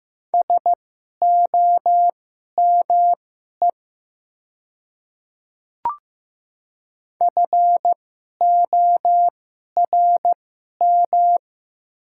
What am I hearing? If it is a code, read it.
Morse code, heard as a keyed tone, about 15 wpm, spaced as for 11 wpm, 700 Hz. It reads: SOME FORM
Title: 2x Character Spacing - Top 300 Words - 15wpm